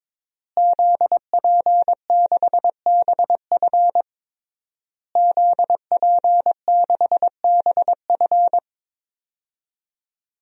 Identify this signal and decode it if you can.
Morse code, heard as a keyed tone, about 22 wpm, 700 Hz. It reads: ZP6BF ZP6BF
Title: DXCC-100 Call Signs - 5 Characters - Repeated 3x - 22wpm